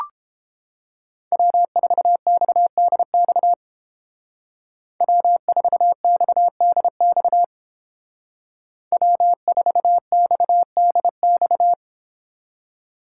Morse code, heard as a keyed tone, about 29 wpm, 700 Hz. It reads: W4XDX W4XDX W4XDX